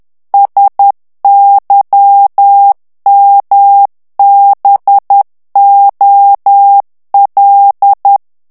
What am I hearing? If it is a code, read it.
Morse code, heard as a keyed tone, about 10 wpm, 800 Hz. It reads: SYMBOL
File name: ANGLAIS texte clair 3 vitesse 10 mots minute